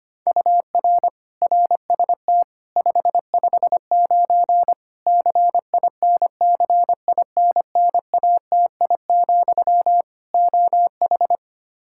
Morse code, heard as a keyed tone, about 25 wpm, 700 Hz. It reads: UR RST 559 CINCINNATI, OH
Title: Practice QSO - Exchange 1 - Beginning - 25wpm